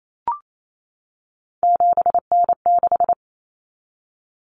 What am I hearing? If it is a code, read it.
Morse code, heard as a keyed tone, about 28 wpm, 700 Hz. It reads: 7N6
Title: Common Cut Numbers - 3-Digit 28wpm